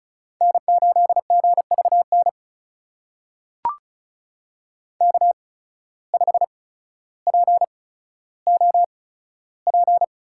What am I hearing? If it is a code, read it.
Morse code, heard as a keyed tone, about 34 wpm, spaced as for 15 wpm, 700 Hz. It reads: N8GVN K5POP
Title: USA Call Signs - 5 Characters - 8x Character Spacing - 35wpm